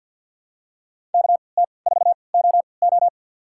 Morse code, heard as a keyed tone, about 53 wpm, spaced as for 29 wpm, 700 Hz. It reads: KT4OO
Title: USA Call Signs - 3x Character Spacing - 50wpm